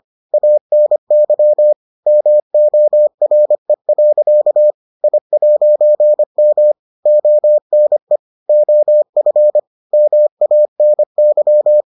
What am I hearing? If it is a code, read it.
Morse code, heard as a keyed tone, about 25 wpm, 600 Hz. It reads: ANY MORE. I'M ONE OF MANY